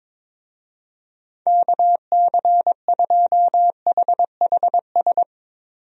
Morse code, heard as a keyed tone, about 22 wpm, 700 Hz. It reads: KC2HHS
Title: USA Call Signs - 6 Characters - Repeated 2x - 22wpm